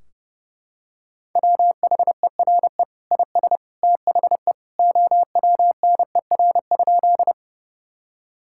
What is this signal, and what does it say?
Morse code, heard as a keyed tone, about 30 wpm, 700 Hz. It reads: WHERE IS THE OWNER?
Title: Sentences from Top 1600 Words 30wpm